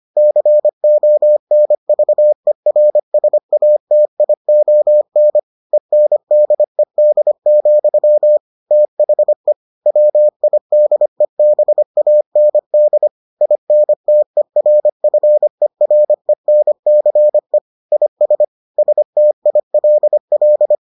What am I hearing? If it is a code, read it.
Morse code, heard as a keyed tone, about 25 wpm, 600 Hz. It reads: CONVERSATION ENDED, THE WIDEBAND INTERFERENCE IS STILL